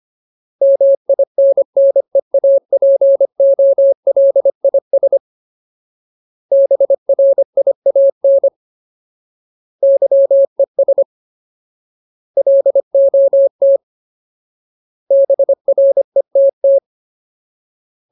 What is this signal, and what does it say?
Morse code, heard as a keyed tone, about 25 wpm, 550 Hz. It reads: MINNEAPOLIS BRIAN YES LOT BRETT
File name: Ultimate Word List / 25WPM Ultimate